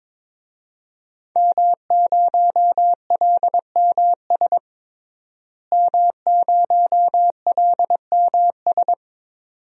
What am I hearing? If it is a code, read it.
Morse code, heard as a keyed tone, about 22 wpm, 700 Hz. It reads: M0LMS M0LMS
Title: DXCC-100 Call Signs - 5 Characters - Repeated 2x - 22wpm